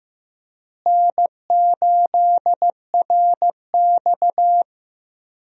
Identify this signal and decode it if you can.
Morse code, heard as a keyed tone, about 15 wpm, 700 Hz. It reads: N8RX
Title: USA Call Signs - 4 Characters - Repeated 3x - 15wpm